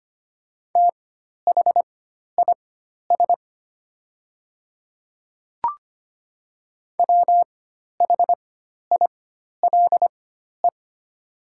Morse code, heard as a keyed tone, about 25 wpm, spaced as for 12 wpm, 700 Hz. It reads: THIS WHILE
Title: 4x Character Spacing - Top 300 Words - 25wpm